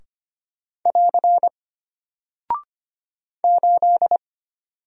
Morse code, heard as a keyed tone, about 25 wpm, 700 Hz. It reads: <AR> 8
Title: Single Character (Letter:Number:Punctuation:Prosign) Alt 25wpm